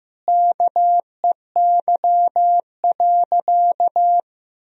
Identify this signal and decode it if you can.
Morse code, heard as a keyed tone, about 15 wpm, 700 Hz. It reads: KEY.
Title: Sentences from Top 1000 Words - Review Part 4 - 15wpm